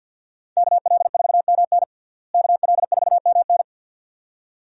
Morse code, heard as a keyed tone, about 47 wpm, 700 Hz. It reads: KL4MN KL4MN
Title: USA Call Signs - 5 Characters - Repeated 2x - 50wpm